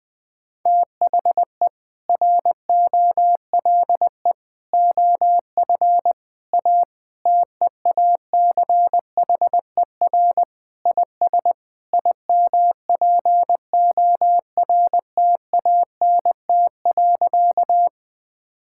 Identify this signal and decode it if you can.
Morse code, heard as a keyed tone, about 20 wpm, 700 Hz. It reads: THE ROLE OF A TEACHER IS IMPORTANT.
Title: Sentences with 3-Word Phrases - 20wpm